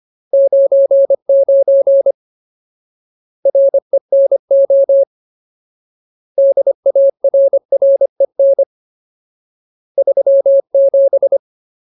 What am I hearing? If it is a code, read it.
Morse code, heard as a keyed tone, about 25 wpm, 550 Hz. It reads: 99 RENO DARREN 37
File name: Ultimate Word List / 25WPM Ultimate